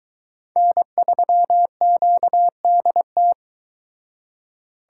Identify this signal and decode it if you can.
Morse code, heard as a keyed tone, about 23 wpm, 700 Hz. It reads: N3QDT